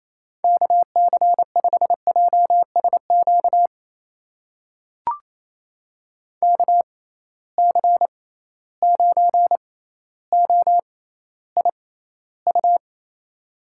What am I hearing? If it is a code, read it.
Morse code, heard as a keyed tone, about 28 wpm, spaced as for 14 wpm, 700 Hz. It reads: KC5JSQ KC9OIU
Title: USA Call Signs - 6x Character Spacing - 28wpm